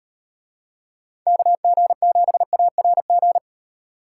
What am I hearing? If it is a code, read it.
Morse code, heard as a keyed tone, about 38 wpm, 700 Hz. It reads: KG7ARG